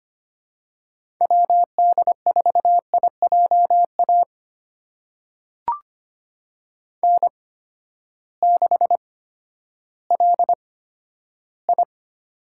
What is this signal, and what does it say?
Morse code, heard as a keyed tone, about 25 wpm, spaced as for 12 wpm, 700 Hz. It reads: WD4IJA N6LI
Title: USA Call Signs - 8x Character Spacing - 25wpm